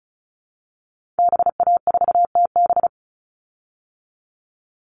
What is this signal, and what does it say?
Morse code, heard as a keyed tone, about 34 wpm, 700 Hz. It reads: BA4TB
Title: Call Signs Repeated 2x - 35wpm